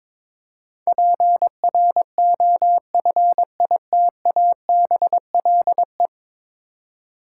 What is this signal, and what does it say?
Morse code, heard as a keyed tone, about 22 wpm, 700 Hz. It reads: PROFITABLE
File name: practice / 22wpm / 10 Letter Words - First 3 Letters Given - 22wpm